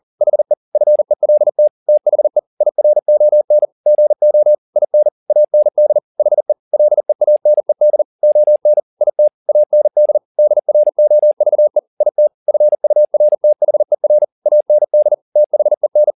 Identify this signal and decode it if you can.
Morse code, heard as a keyed tone, about 40 wpm, 600 Hz. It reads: HE FELT THE IRON GO IN AND HE LEANED ON IT AND DROVE IT FURTHER AND THEN
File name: the old man and the sea 2 40wpm